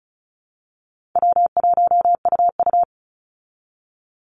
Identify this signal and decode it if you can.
Morse code, heard as a keyed tone, about 34 wpm, 700 Hz. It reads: W1UU
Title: Easy Call Signs Repeated 3x - 35wpm